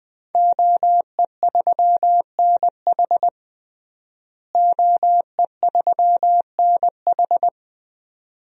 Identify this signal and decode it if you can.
Morse code, heard as a keyed tone, about 19 wpm, 700 Hz. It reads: OE3NH OE3NH
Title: DXCC-100 Call Signs - 5 Characters - Repeated 2x - 20wpm